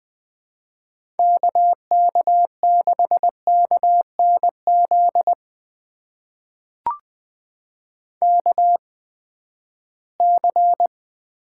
Morse code, heard as a keyed tone, about 21 wpm, spaced as for 14 wpm, 700 Hz. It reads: KK6KNZ KC
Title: USA Call Signs - 6 Characters - 8x Character Spacing - 20wpm